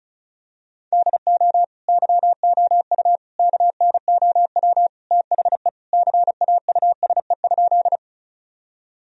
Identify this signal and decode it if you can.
Morse code, heard as a keyed tone, about 34 wpm, 700 Hz. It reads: DO YOU KNOW THE CAUSE?
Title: Sentences from Top 1000 Words - Review Part 1 - 35wpm